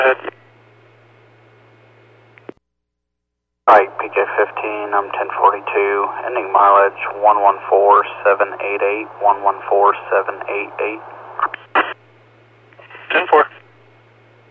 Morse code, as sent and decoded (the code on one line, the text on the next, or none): none